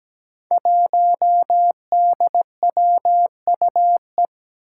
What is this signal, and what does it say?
Morse code, heard as a keyed tone, about 17 wpm, 700 Hz. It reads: 1DWUE
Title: Five Challenging Letter-Number Combinations 17wpm